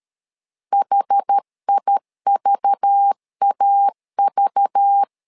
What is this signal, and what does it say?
Morse code, heard as a keyed tone, about 13 wpm, 800 Hz. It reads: HIVAV
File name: LEZIONI / 5 LBVC / 5 LBV 12wpm 1855